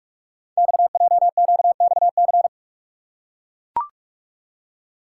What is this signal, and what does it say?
Morse code, heard as a keyed tone, about 44 wpm, 690 Hz. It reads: XJQXC
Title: Code Groups - 5 Letters - 45wpm